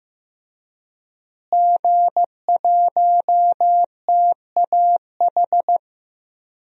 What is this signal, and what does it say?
Morse code, heard as a keyed tone, about 15 wpm, 700 Hz. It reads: G1TAH
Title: DXCC-100 Call Signs - 5 Characters - Repeated 3x - 15wpm